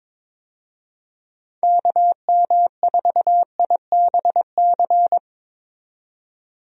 Morse code, heard as a keyed tone, about 21 wpm, 700 Hz. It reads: KM4IBC
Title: USA Call Signs - 6 Characters - 22wpm